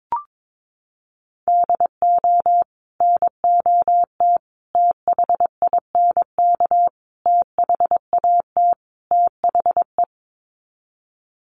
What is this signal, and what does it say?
Morse code, heard as a keyed tone, about 22 wpm, 700 Hz. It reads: DO NOT THINK THAT THE
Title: Sets of 5 Words Encore2 22wpm